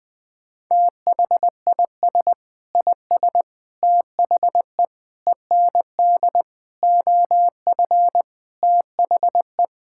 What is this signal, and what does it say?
Morse code, heard as a keyed tone, about 21 wpm, 700 Hz. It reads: THIS IS THE END OF THE
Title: Sentences with 9-Word Phrases - Spoken Beginning - 20wpm